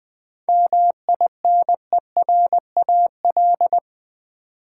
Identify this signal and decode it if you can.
Morse code, heard as a keyed tone, about 20 wpm, 700 Hz. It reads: MINERAL
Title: 7 Letter Words - First 3 Letters Given - 20wpm